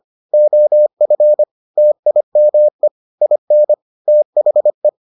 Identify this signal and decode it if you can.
Morse code, heard as a keyed tone, about 25 wpm, 600 Hz. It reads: OF TIME IN THE